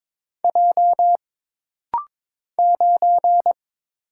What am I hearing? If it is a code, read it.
Morse code, heard as a keyed tone, about 22 wpm, 700 Hz. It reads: J 9
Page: This rapid-fire timing is geared to push your ICR (Instant Character Recognition) skills.